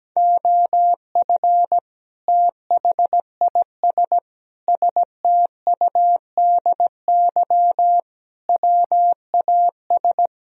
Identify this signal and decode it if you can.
Morse code, heard as a keyed tone, about 17 wpm, 700 Hz. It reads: OF THIS STUDY WAS
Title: Speed-Racing - Sets of 5 Words 17wpm